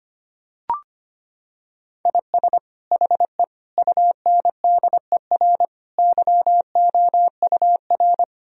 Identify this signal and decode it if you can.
Morse code, heard as a keyed tone, about 25 wpm, 700 Hz. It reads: IS HE UNDER YOUR
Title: Sentences from Top 1000 Words - Review Part 1 - 25wpm